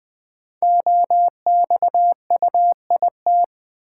Morse code, heard as a keyed tone, about 20 wpm, 700 Hz. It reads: OXUIT
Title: Code Groups - 5 Letters - 20wpm